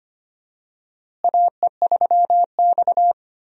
Morse code, heard as a keyed tone, about 25 wpm, 700 Hz. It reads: AE3X
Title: USA Call Signs - 4 Characters - Repeated 2x - 25wpm